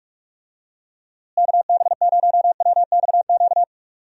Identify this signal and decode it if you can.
Morse code, heard as a keyed tone, about 46 wpm, 700 Hz. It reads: KB0WXQ
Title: USA Call Signs - 6 Characters - Repeated 2x - 45wpm